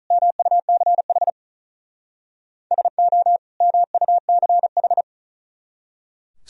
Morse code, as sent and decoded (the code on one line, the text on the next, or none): -- ..- -.-. .... / ... --- / -- ..- -.-. ....
MUCH SO MUCH